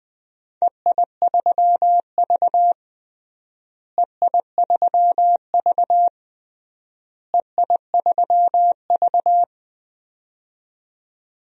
Morse code, heard as a keyed tone, about 21 wpm, 700 Hz. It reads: EI3V EI3V EI3V
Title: DXCC-100 Call Signs - 4 Characters - Repeated 3x - 20wpm